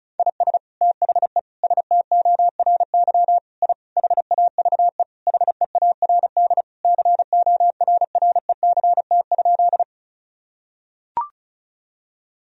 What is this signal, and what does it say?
Morse code, heard as a keyed tone, about 36 wpm, 700 Hz. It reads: IS THE STORY I HAVE HEARD CORRECT?
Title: Sentences from Top 1000 Words - Review Part 2 - 35wpm